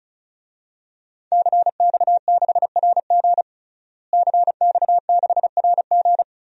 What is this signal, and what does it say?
Morse code, heard as a keyed tone, about 36 wpm, 700 Hz. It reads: CX6RG CX6RG
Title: DXCC-100 Call Signs - 5 Characters - Repeated 2x - 35wpm